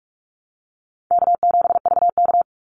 Morse code, heard as a keyed tone, about 44 wpm, 700 Hz. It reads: K7VK
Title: NA SOTA Activator Callsigns 45wpm